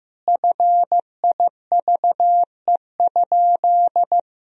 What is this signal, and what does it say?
Morse code, heard as a keyed tone, about 15 wpm, 700 Hz. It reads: FIVE?
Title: Sentences from Top 1000 Words - Review Part 2 - 15wpm